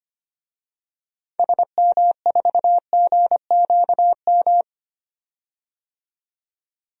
Morse code, heard as a keyed tone, about 25 wpm, 700 Hz. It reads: SM4GQM
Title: DXCC-100 Call Signs - 6 Characters - 25wpm